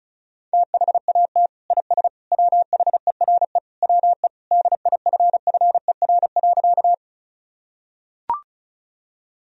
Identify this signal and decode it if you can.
Morse code, heard as a keyed tone, about 34 wpm, 700 Hz. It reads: THAT IS WHERE WE DIFFER.
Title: Sentences from Top 1000 Words - Review Part 1 - 35wpm